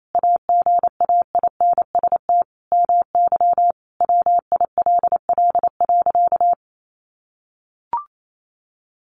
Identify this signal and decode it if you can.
Morse code, heard as a keyed tone, about 28 wpm, 700 Hz. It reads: AGAINST MY WILL.
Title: Sentences from Top 1400 Words 28wpm